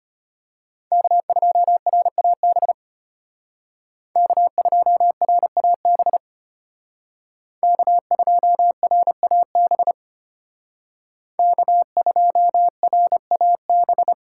Morse code, heard as a keyed tone, about 32 wpm, 700 Hz. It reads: K2RAB K2RAB K2RAB K2RAB